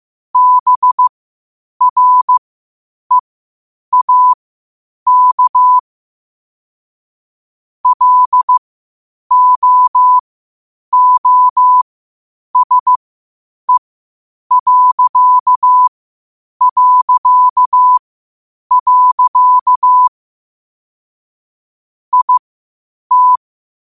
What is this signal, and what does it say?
Morse code, heard as a keyed tone, about 15 wpm, spaced as for 8 wpm, 1000 Hz. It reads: BREAK LOOSE... IT
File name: qotdfiles / 8wpm